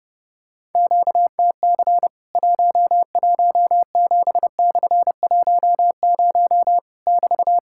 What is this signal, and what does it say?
Morse code, heard as a keyed tone, about 30 wpm, 700 Hz. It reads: QTC 117/10 <BT>
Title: Worked All Europe DX Contest - QTC Exchange - 30wpm